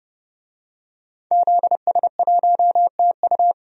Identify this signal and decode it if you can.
Morse code, heard as a keyed tone, about 30 wpm, 700 Hz. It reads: ZS1TU
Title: DXCC-100 Call Signs - 5 Characters - 30wpm